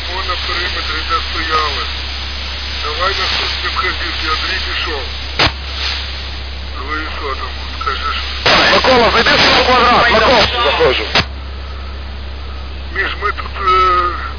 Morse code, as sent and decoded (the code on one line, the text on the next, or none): none